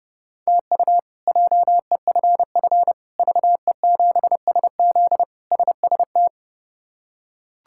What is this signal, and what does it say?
Morse code, heard as a keyed tone, about 32 wpm, 700 Hz. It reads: TU JEFF VE7SZ SST
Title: K1USN SST QSO Practice - 30wpm